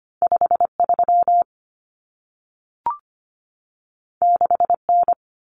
Morse code, heard as a keyed tone, about 25 wpm, 700 Hz. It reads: 53 6N
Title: Common Cut Numbers - 2-Digit 25wpm